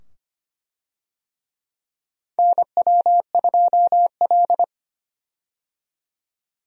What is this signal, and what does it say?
Morse code, heard as a keyed tone, about 25 wpm, 700 Hz. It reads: NW2L
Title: USA Call Signs - 4 Characters - 25wpm